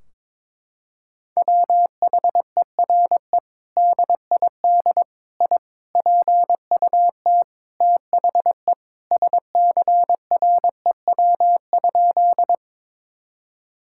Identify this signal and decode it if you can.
Morse code, heard as a keyed tone, about 22 wpm, 700 Hz. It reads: WHERE DID I PUT THE SCREW?
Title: Sentences from Top 1600 Words 22wpm